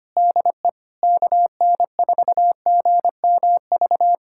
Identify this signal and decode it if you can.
Morse code, heard as a keyed tone, about 25 wpm, 700 Hz. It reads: DE KN4GMV
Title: USA Callsign DE Callsign 25wpm